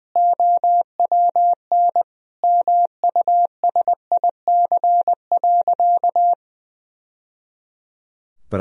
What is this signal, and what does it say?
Morse code, heard as a keyed tone, about 20 wpm, 700 Hz. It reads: OWN MUSIC.
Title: Sentences from Top 1000 Words - Review Part 3 - 20wpm